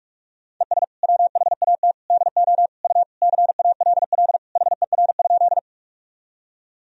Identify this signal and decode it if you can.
Morse code, heard as a keyed tone, about 46 wpm, 700 Hz. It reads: ES WHAT DO U CALL HER?
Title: Sentences from Top 100 Words and Abbreviations 45wpm